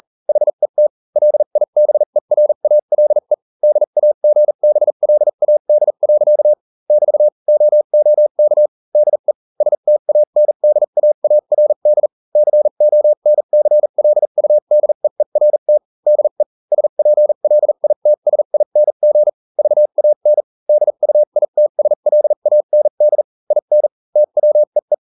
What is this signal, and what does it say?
Morse code, heard as a keyed tone, about 40 wpm, spaced as for 35 wpm, 600 Hz. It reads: HET LIBERALE DAGBLAD. <BT> OOK DE STANDAARD CONCLUDEERT DE SPLITSING VAN DUITSLAND IN TWEE